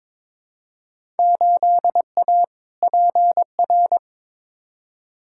Morse code, heard as a keyed tone, about 22 wpm, 700 Hz. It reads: 8A PR
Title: Field Day Exchanges Encore - 22wpm